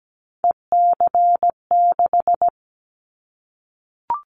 Morse code, heard as a keyed tone, about 17 wpm, 700 Hz. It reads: EC6
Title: Callsign Prefixes for Top 100 DXCC - 17wpm